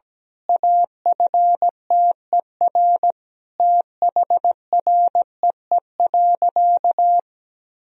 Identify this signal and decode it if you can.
Morse code, heard as a keyed tone, about 17 wpm, 700 Hz. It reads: AFTER THREE.
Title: Sentences from Top 1000 Words - Review Part 1 - 17wpm